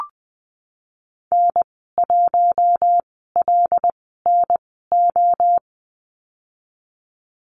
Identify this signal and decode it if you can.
Morse code, heard as a keyed tone, about 20 wpm, spaced as for 14 wpm, 700 Hz. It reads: N1LNO
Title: 088 Lesson 15g - Number 5 - Call Signs 20wpm